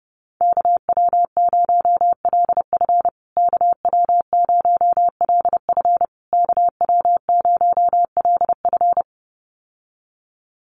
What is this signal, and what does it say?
Morse code, heard as a keyed tone, about 30 wpm, 700 Hz. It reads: KW0LF KW0LF KW0LF
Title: USA Call Signs Repeated 3x 30wpm